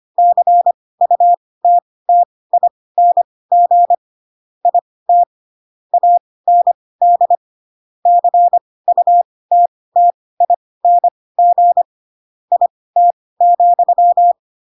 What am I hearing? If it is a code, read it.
Morse code, heard as a keyed tone, about 25 wpm, spaced as for 18 wpm, 700 Hz. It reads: CUTTING IT AND CUTTING IT,
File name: Pinocchio ch3 12 25 ik1hge